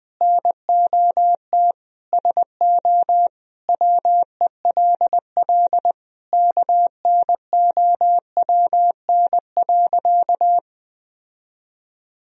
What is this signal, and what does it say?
Morse code, heard as a keyed tone, about 20 wpm, 700 Hz. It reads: NOT SO WELL KNOWN.
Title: Sentences from Top 1000 Words - Review Part 3 - 20wpm